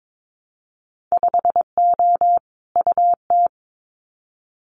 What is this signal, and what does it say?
Morse code, heard as a keyed tone, about 22 wpm, 700 Hz. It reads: 5O UT
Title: Winter Field Day Exchanges 22wpm